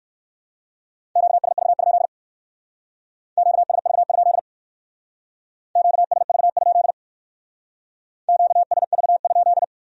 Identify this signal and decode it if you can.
Morse code, heard as a keyed tone, about 60 wpm, 700 Hz. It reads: QSV? QSV? QSV? QSV?